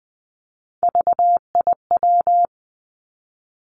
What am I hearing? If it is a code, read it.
Morse code, heard as a keyed tone, about 20 wpm, 700 Hz. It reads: VIW
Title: Three Challenging Letter-Number Combinations 20wpm